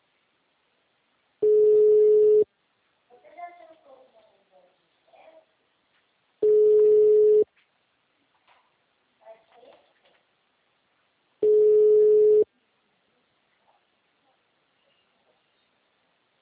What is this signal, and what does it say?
Ringback tone (Europe, 425 Hz), with speech.
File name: out- -2017- - - .5291